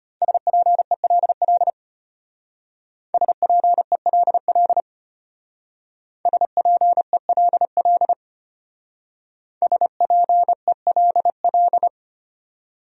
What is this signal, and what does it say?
Morse code, heard as a keyed tone, about 31 wpm, 700 Hz. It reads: SPELL SPELL SPELL SPELL
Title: Speed-Racing Top 200 Words 25wpm